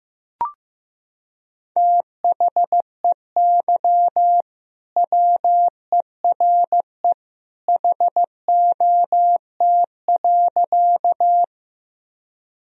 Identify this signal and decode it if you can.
Morse code, heard as a keyed tone, about 15 wpm, 700 Hz. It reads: THEY WERE HOT.
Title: Sentences from Top 1000 Words - Review Part 3 - 15wpm